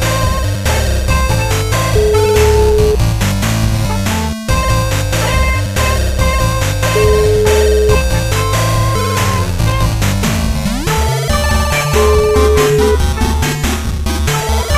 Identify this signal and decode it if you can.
Ringback tone (Europe, 425 Hz), with noise.